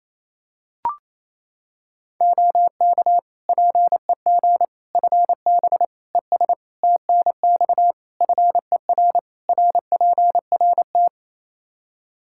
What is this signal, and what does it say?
Morse code, heard as a keyed tone, about 28 wpm, 700 Hz. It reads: OK PEG FB ES TNX FER RPRT
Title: Practice QSO - Exchange 2 - Info - 28wpm